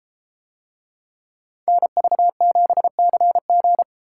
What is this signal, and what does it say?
Morse code, heard as a keyed tone, about 31 wpm, 700 Hz. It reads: NV7CG